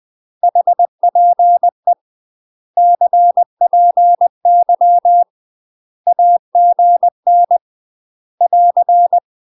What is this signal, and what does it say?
Morse code, heard as a keyed tone, about 20 wpm, 700 Hz. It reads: HPE CPY AGN <AR>
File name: UK MASTER QSOs TRAINING / 20WPM / UK MASTER QSOs 0015